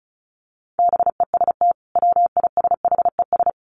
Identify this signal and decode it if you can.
Morse code, heard as a keyed tone, about 34 wpm, 700 Hz. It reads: BEST WISHES
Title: Common QSO Phrases 35wpm